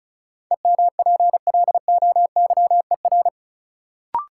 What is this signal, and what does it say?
Morse code, heard as a keyed tone, about 36 wpm, 700 Hz. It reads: EMPLOYER
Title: Sentences from Top 1600 Words - New Word Drill - 35wpm